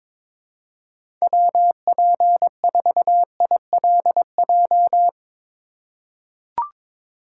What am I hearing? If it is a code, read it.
Morse code, heard as a keyed tone, about 22 wpm, 700 Hz. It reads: WP4ILJ
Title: USA Call Signs - 6 Characters - 6x Character Spacing - 22wpm